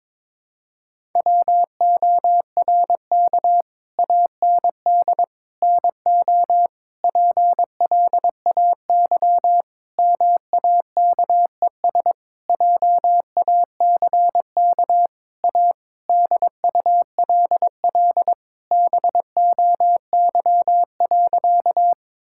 Morse code, heard as a keyed tone, about 21 wpm, 700 Hz. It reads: WORK AND NO PLAY MAKES JACK A DULL BOY.
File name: Sentences with 7-Word Phrases - 22wpm